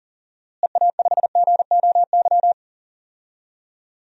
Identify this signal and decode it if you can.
Morse code, heard as a keyed tone, about 40 wpm, 700 Hz. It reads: EA5GOY
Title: DXCC-100 Call Signs - 6 Characters - Repeated 3x - 40wpm